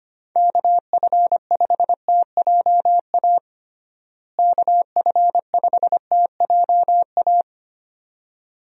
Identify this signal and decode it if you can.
Morse code, heard as a keyed tone, about 25 wpm, 700 Hz. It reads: KF5TJA KF5TJA
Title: USA Call Signs - 6 Characters - Repeated 2x - 25wpm